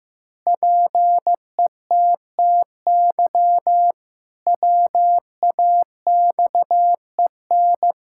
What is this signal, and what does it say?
Morse code, heard as a keyed tone, about 15 wpm, 700 Hz. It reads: PETTY WAXEN
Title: Pangram Challenge v2 - 15wpm